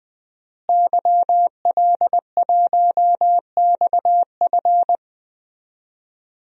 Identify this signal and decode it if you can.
Morse code, heard as a keyed tone, about 21 wpm, 700 Hz. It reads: YL1XF
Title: DXCC-100 Call Signs - 5 Characters - Repeated 3x - 20wpm